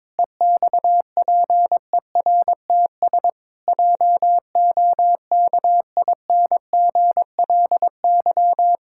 Morse code, heard as a keyed tone, about 22 wpm, 700 Hz. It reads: EXPERTS JOKINGLY
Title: Pangram Challenge v2 - 22wpm